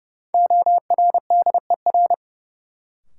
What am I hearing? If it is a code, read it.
Morse code, heard as a keyed tone, about 30 wpm, 700 Hz. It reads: ORDER
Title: Sentences from Top 400 Words - New Word Drill - 30wpm